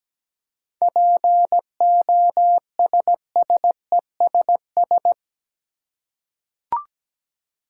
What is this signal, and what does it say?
Morse code, heard as a keyed tone, about 17 wpm, 700 Hz. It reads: POSSESS
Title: Seven Letter Words - 17wpm